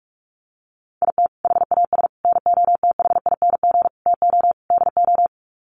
Morse code, heard as a keyed tone, about 44 wpm, 700 Hz. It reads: IT HAS NOTHING TO DO
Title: Sets of 5 Words Encore 45wpm